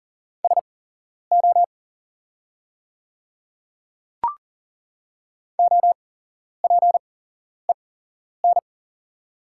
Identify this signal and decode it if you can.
Morse code, heard as a keyed tone, about 38 wpm, spaced as for 11 wpm, 690 Hz. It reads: SO OPEN
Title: 8x Character Spacing - Top 300 Words - 40wpm